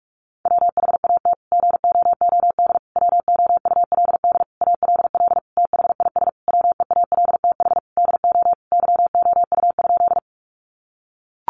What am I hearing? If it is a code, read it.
Morse code, heard as a keyed tone, about 44 wpm, 700 Hz. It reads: WHAT GOOD WOULD ALL THIS WEALTH DO YOU?
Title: Sentences from Top 1200 Words 45wpm